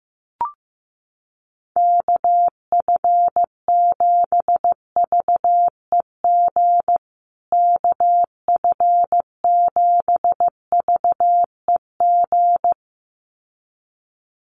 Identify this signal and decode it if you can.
Morse code, heard as a keyed tone, about 15 wpm, 700 Hz. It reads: KF7VEG KF7VEG
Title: USA Call Signs Repeated 2x 15wpm